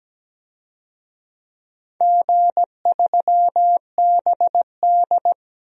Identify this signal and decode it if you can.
Morse code, heard as a keyed tone, about 17 wpm, 700 Hz. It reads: G3BD
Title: DXCC-100 Call Signs - 4 Characters - Repeated 2x - 17wpm